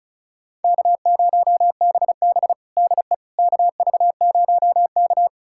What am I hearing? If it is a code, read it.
Morse code, heard as a keyed tone, about 36 wpm, 700 Hz. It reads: K0BB DE KV0K
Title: USA Callsign DE Callsign - 4 Characters - 35wpm